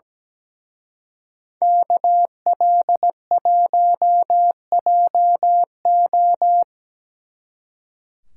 Morse code, heard as a keyed tone, about 20 wpm, 700 Hz. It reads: KL1JO
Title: Speed Racing - USA 2x2 Call Signs 17wpm